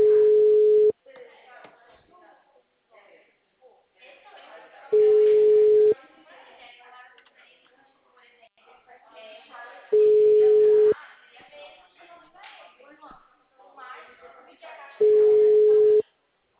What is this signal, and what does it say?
Ringback tone (Europe, 425 Hz), with speech.